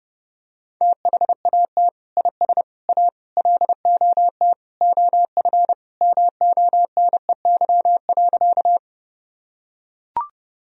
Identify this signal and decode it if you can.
Morse code, heard as a keyed tone, about 30 wpm, 700 Hz. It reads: THAT IS A LOT OF MONEY.
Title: Sentences from Top 1000 Words - Review Part 3 - 30wpm